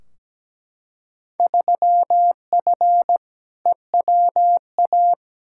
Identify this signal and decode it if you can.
Morse code, heard as a keyed tone, about 17 wpm, 700 Hz. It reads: 3F EWA
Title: Field Day Exchanges Encore - 17wpm